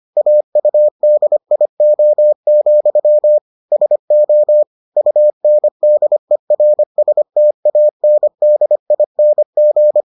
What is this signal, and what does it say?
Morse code, heard as a keyed tone, about 25 wpm, 600 Hz. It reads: AUDIO, SO UNDERSTANDING